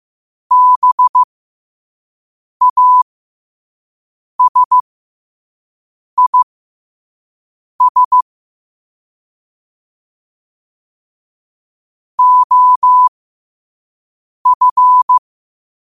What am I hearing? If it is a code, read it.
Morse code, heard as a keyed tone, about 15 wpm, spaced as for 5 wpm, 1000 Hz. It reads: BASIS OF
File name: qotdfiles / 5wpm